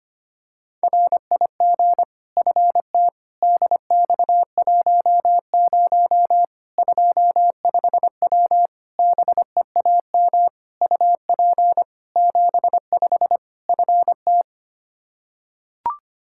Morse code, heard as a keyed tone, about 25 wpm, 700 Hz. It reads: RIG FT DX10 25W BEAM UP 75 FT
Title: Practice QSO - Exchange 2 - Info - 25wpm